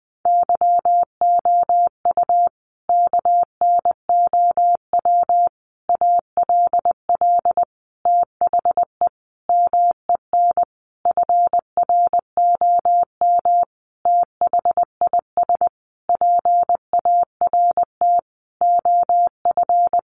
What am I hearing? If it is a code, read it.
Morse code, heard as a keyed tone, about 20 wpm, 700 Hz. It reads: YOU KNOW ALL THE MEN FROM THIS PART OF
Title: Sentences from Top 300 Words 20wpm